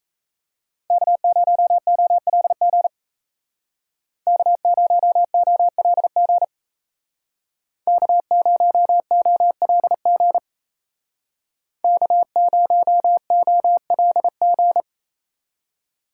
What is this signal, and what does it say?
Morse code, heard as a keyed tone, about 35 wpm, 700 Hz. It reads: K0OLG K0OLG K0OLG K0OLG